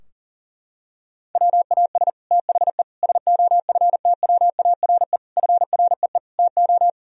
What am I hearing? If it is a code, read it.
Morse code, heard as a keyed tone, about 37 wpm, 700 Hz. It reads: WAS THE SOFTWARE FREE TO
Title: Sentences from Top 1000 Words - Review Part 3 - 40wpm